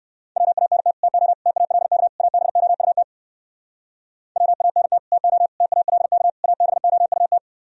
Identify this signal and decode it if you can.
Morse code, heard as a keyed tone, about 58 wpm, 700 Hz. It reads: WANT TO TALK ABOUT WANT TO TALK ABOUT